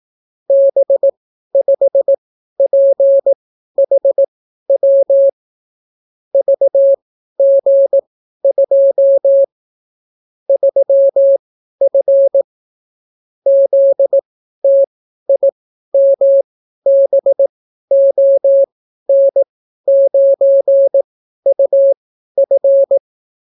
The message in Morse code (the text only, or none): B5PHW VG2 3F ZTIMBON9UF